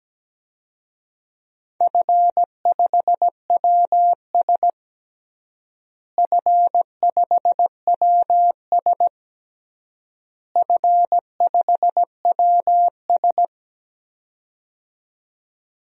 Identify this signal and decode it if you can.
Morse code, heard as a keyed tone, about 17 wpm, 700 Hz. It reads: F5WS F5WS F5WS
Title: DXCC-100 Call Signs - 4 Characters - Repeated 3x - 17wpm